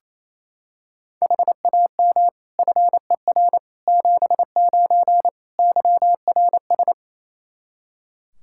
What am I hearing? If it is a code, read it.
Morse code, heard as a keyed tone, about 27 wpm, 700 Hz. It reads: HAM FER 79 YRS
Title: Practice QSO - Exchange 3 - More Info - 28wpm